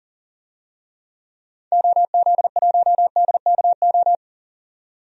Morse code, heard as a keyed tone, about 40 wpm, 700 Hz. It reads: OZ1DKO
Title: DXCC-100 Call Signs - 6 Characters - 40wpm